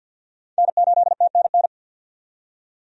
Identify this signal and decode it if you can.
Morse code, heard as a keyed tone, about 52 wpm, 700 Hz. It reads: N8TNN
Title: USA Call Signs - 5 Characters - 50wpm